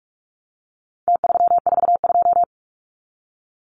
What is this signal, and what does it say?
Morse code, heard as a keyed tone, about 46 wpm, 700 Hz. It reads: T342
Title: Numbers - Contest Style - 4-Digit 45wpm